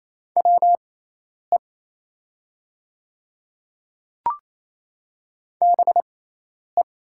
Morse code, heard as a keyed tone, about 28 wpm, spaced as for 10 wpm, 700 Hz. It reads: WE BE
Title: 6x Character Spacing - Top 100 Words - 28wpm